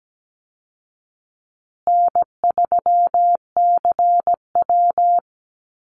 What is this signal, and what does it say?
Morse code, heard as a keyed tone, about 17 wpm, 700 Hz. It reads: N3CW
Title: Easy Call Signs Repeated 3x - 17wpm